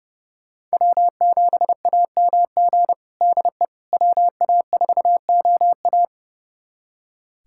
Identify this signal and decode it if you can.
Morse code, heard as a keyed tone, about 30 wpm, 700 Hz. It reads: W7AMG DE WA4OA
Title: USA Callsign DE Callsign - 5 Characters - 30wpm